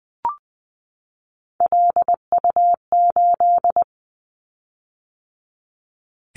Morse code, heard as a keyed tone, about 20 wpm, 700 Hz. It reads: LU8
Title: Callsign Prefixes for Top 50 DXCC - 20wpm